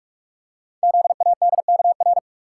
Morse code, heard as a keyed tone, about 44 wpm, 700 Hz. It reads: ZADKR
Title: Code Groups - 5 Letters - 45wpm